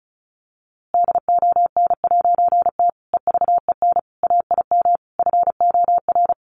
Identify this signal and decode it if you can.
Morse code, heard as a keyed tone, about 34 wpm, 700 Hz. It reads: DON'T EVEN AIM FOR
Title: A Sound Of Thunder 35wpm